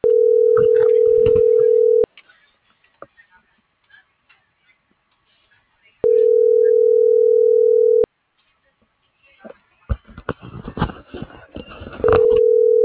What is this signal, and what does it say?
Ringback tone (North America), 440 + 480 Hz.